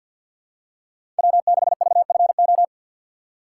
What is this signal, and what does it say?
Morse code, heard as a keyed tone, about 49 wpm, 700 Hz. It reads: W6VFO
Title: USA Call Signs - 6x Character Spacing - 50wpm